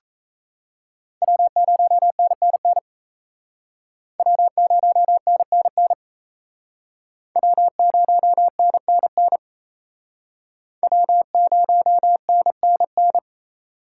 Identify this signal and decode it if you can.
Morse code, heard as a keyed tone, about 38 wpm, 700 Hz. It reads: W0NNN W0NNN W0NNN W0NNN